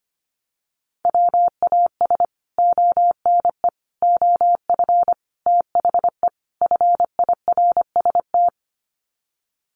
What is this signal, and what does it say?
Morse code, heard as a keyed tone, about 25 wpm, 700 Hz. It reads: WAS ONE OF THE FIRST
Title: Sets of 5 Words 25wpm